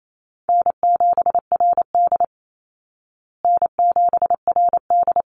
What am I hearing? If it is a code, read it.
Morse code, heard as a keyed tone, about 28 wpm, 700 Hz. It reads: N7RD N7RD
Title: Call Signs Repeated 3x - 28wpm